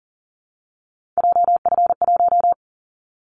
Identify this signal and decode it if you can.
Morse code, heard as a keyed tone, about 40 wpm, 700 Hz. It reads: JF1
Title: Callsign Prefixes for Top 25 DXCC - Repeated 3x - 40wpm